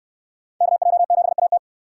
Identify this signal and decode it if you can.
Morse code, heard as a keyed tone, about 68 wpm, 700 Hz. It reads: KJ7AT